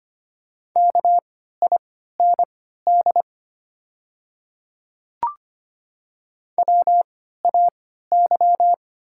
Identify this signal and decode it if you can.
Morse code, heard as a keyed tone, about 25 wpm, spaced as for 14 wpm, 690 Hz. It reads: KIND WAY
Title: 3x Character Spacing - Top 200 Words - 25wpm